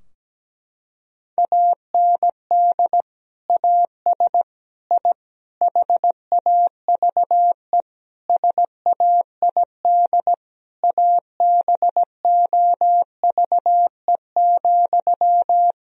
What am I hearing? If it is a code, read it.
Morse code, heard as a keyed tone, about 17 wpm, 700 Hz. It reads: AND AS I HAVE SAID ABOVE,
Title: Sentences from Top 1000 Words - Review Part 2 - 17wpm